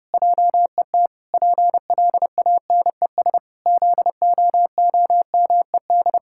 Morse code, heard as a keyed tone, about 30 wpm, 700 Hz. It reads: JET PLANES ZOOMED
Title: Pangram Challenge v2 - 30wpm